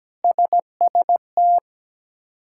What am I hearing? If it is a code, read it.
Morse code, heard as a keyed tone, about 17 wpm, 700 Hz. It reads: SST
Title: K1USN SST QSO Practice - 17wpm